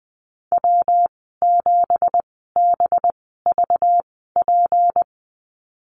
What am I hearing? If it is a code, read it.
Morse code, heard as a keyed tone, about 20 wpm, spaced as for 14 wpm, 700 Hz. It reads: W7BVP
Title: 213 Lesson 34g - Number 8 - Call Signs 20wpm